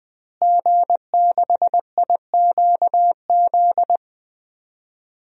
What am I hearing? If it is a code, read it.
Morse code, heard as a keyed tone, about 20 wpm, 700 Hz. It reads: G6IQZ
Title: DXCC-100 Call Signs - 5 Characters - Repeated 2x - 20wpm